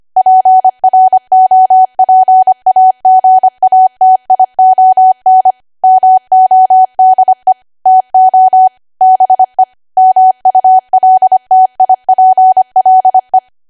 Morse code, heard as a keyed tone, about 25 wpm, 750 Hz. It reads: PROPAGATION MODE TO BE MULTIPLE